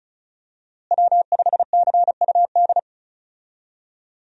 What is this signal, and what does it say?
Morse code, heard as a keyed tone, about 36 wpm, 700 Hz. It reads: W5CUD
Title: USA Call Signs - 5 Characters - Repeated 2x - 35wpm